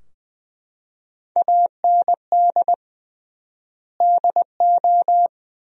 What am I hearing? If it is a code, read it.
Morse code, heard as a keyed tone, about 20 wpm, 700 Hz. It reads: AND DO
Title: 3x Word-Spacing - Sentences from Top 200 Words - 20wpm